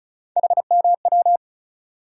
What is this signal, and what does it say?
Morse code, heard as a keyed tone, about 36 wpm, 700 Hz. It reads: HMW
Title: USA Call Sign Postfixes - 3 Letters 35wpm